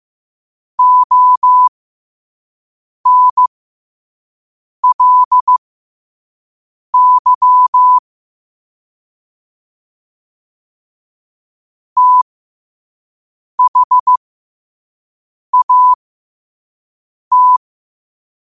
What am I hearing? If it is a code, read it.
Morse code, heard as a keyed tone, about 15 wpm, spaced as for 5 wpm, 1000 Hz. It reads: ONLY THAT
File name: qotdfiles / 5wpm